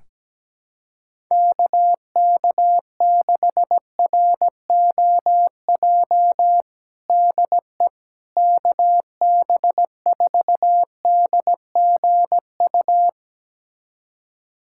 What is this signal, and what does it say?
Morse code, heard as a keyed tone, about 17 wpm, 700 Hz. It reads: KK6ROJ DE KB4DGU
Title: USA Callsign DE Callsign - 6 Characters - 17wpm